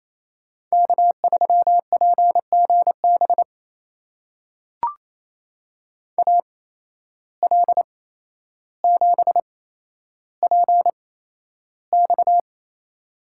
Morse code, heard as a keyed tone, about 28 wpm, spaced as for 12 wpm, 700 Hz. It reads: K3PGB AL7PX
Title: USA Call Signs - 5 Characters - 8x Character Spacing - 28wpm